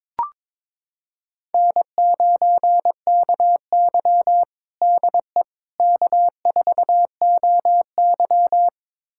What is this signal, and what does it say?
Morse code, heard as a keyed tone, about 21 wpm, 700 Hz. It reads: N9KY DE K4OY
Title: USA Callsign DE Callsign - 4 Characters - 22wpm